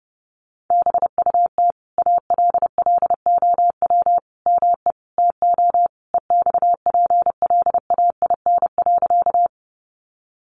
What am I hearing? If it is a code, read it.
Morse code, heard as a keyed tone, about 30 wpm, 700 Hz. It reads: BUT ALLOW ME TO EXPLAIN.
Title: Sentences from Top 1100 Words 30wpm